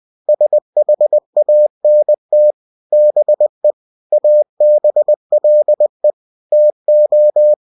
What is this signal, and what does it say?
Morse code, heard as a keyed tone, about 20 wpm, 600 Hz. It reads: SHANT BE ABLE TO
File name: cw-books / winnie-the-pooh / winnie-20wpm-page-0024